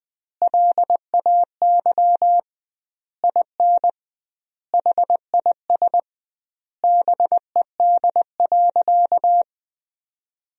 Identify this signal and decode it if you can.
Morse code, heard as a keyed tone, about 20 wpm, 700 Hz. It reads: LAY IN HIS BED.
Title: 2x Word-Spacing - Sentences from Top 500 Words - 20wpm